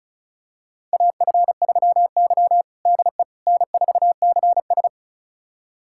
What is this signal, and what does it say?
Morse code, heard as a keyed tone, about 36 wpm, 700 Hz. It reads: AF3Y DE N4CS
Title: USA Callsign DE Callsign - 4 Characters - 35wpm